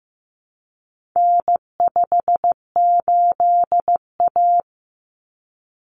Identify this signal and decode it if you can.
Morse code, heard as a keyed tone, about 15 wpm, 700 Hz. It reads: N58A
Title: Common Cut Numbers - 4-Digit 15wpm